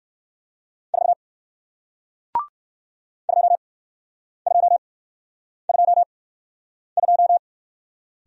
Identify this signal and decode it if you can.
Morse code, heard as a keyed tone, about 58 wpm, spaced as for 9 wpm, 700 Hz. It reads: 4 2222